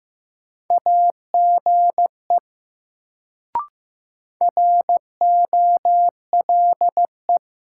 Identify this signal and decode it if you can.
Morse code, heard as a keyed tone, about 15 wpm, 700 Hz. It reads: AGE ROLE
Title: Sets of 3 Words - New Word Drill - 15wpm